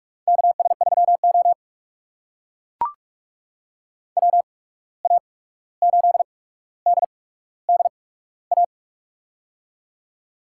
Morse code, heard as a keyed tone, about 45 wpm, spaced as for 17 wpm, 700 Hz. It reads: KS3O WA8DDA
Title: USA Call Signs - 8x Character Spacing - 45wpm